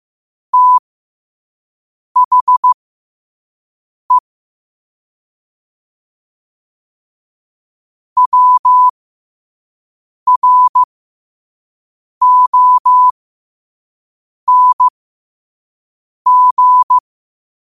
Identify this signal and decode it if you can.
Morse code, heard as a keyed tone, about 15 wpm, spaced as for 5 wpm, 1000 Hz. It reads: THE WRONG
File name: qotdfiles / 5wpm